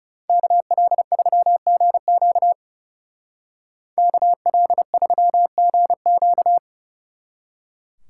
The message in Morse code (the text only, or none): KL3GQ KL3GQ